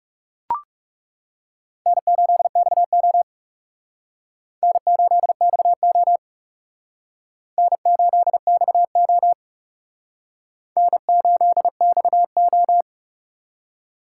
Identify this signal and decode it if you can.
Morse code, heard as a keyed tone, about 37 wpm, 700 Hz. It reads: N8XO N8XO N8XO N8XO
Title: Speed Racing - USA 1x2 Call Signs 30wpm